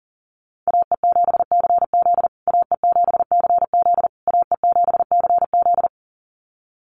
Morse code, heard as a keyed tone, about 40 wpm, 700 Hz. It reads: AE7CZ AE7CZ AE7CZ
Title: USA Call Signs Repeated 3x 40wpm